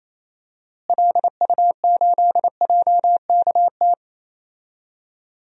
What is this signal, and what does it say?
Morse code, heard as a keyed tone, about 28 wpm, 700 Hz. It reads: LU8JKT
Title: DXCC-100 Call Signs - 6 Characters - Repeated 3x - 28wpm